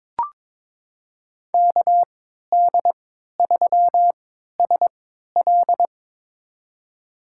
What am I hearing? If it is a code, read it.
Morse code, heard as a keyed tone, about 23 wpm, spaced as for 13 wpm, 700 Hz. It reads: KD3SL
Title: USA Call Signs - 5 Characters - 3x Character Spacing - 22wpm